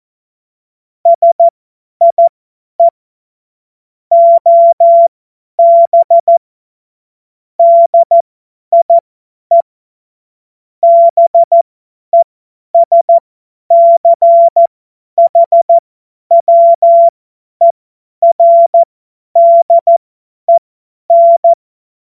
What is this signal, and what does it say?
Morse code, heard as a keyed tone, about 14 wpm, spaced as for 10 wpm, 680 Hz. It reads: SIE OB DIE BESCHWERDEN
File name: Morsekurs / Witze / Witz 27. Juli 2021, 50 F70